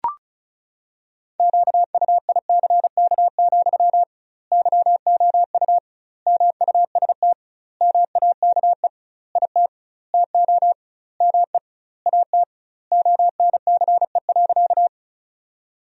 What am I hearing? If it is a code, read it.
Morse code, heard as a keyed tone, about 36 wpm, 700 Hz. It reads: QUICK, YOU MUST MAKE IT TO ME AT ONCE.
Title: 2x Word-Spacing - Sentences from Top 500 Words - 35wpm